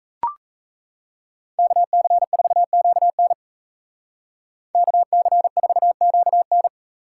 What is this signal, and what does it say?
Morse code, heard as a keyed tone, about 33 wpm, 700 Hz. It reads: KC4QN KC4QN
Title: Speed Racing - USA Call Signs 28wpm